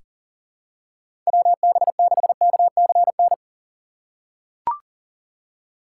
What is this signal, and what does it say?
Morse code, heard as a keyed tone, about 37 wpm, 700 Hz. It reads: WB6KCN
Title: USA Call Signs - 6 Characters - 3x Character Spacing - 40wpm